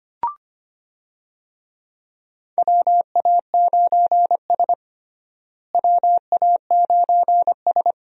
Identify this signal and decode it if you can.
Morse code, heard as a keyed tone, about 25 wpm, 700 Hz. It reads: WA9S WA9S
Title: USA Call Signs - 4 Characters - Repeated 3x - 25wpm